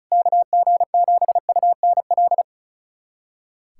Morse code, heard as a keyed tone, about 34 wpm, 700 Hz. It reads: KG7UNL
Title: USA Callsign DE Callsign - 6 Characters - 35wpm